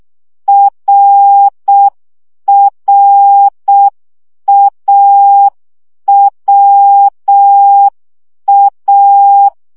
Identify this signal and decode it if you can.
Morse code, heard as a keyed tone, about 6 wpm, 800 Hz. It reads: RRAWA
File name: CW gyakorlas a r w 5 baud